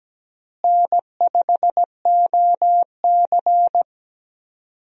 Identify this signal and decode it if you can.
Morse code, heard as a keyed tone, about 17 wpm, 700 Hz. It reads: N5OC
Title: USA Call Signs - 4 Characters - Repeated 2x - 17wpm